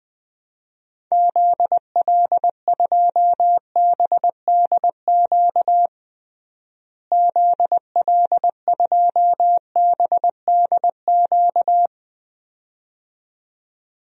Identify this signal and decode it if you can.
Morse code, heard as a keyed tone, about 20 wpm, 700 Hz. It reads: ZL2BDQ ZL2BDQ
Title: DXCC-100 Call Signs - 6 Characters - Repeated 2x - 20wpm